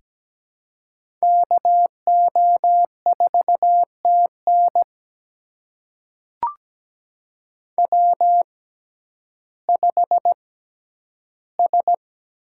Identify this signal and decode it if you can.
Morse code, heard as a keyed tone, about 17 wpm, spaced as for 10 wpm, 700 Hz. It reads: KO4TN W5S
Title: USA Call Signs - 5 Characters - 6x Character Spacing - 17wpm